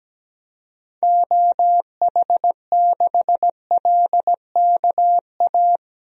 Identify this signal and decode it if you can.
Morse code, heard as a keyed tone, about 17 wpm, 700 Hz. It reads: OH6LKA
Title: DXCC-100 Call Signs - 6 Characters - Repeated 3x - 17wpm